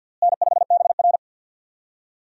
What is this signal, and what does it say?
Morse code, heard as a keyed tone, about 47 wpm, 700 Hz. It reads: N5BR
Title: USA Call Signs - 4 Characters - Repeated 3x - 50wpm